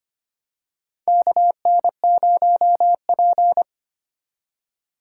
Morse code, heard as a keyed tone, about 25 wpm, 700 Hz. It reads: KN0P
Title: USA Call Signs - 4 Characters - 6x Character Spacing - 25wpm